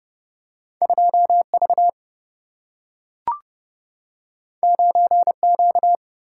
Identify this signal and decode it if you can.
Morse code, heard as a keyed tone, about 30 wpm, 700 Hz. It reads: 2 V 9 Q
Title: Single-Letter Single-Number 30wpm